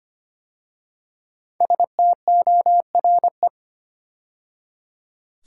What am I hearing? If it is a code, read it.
Morse code, heard as a keyed tone, about 25 wpm, 700 Hz. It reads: STORE
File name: 5 Letter Words - First 3 Letters Given - 25wpm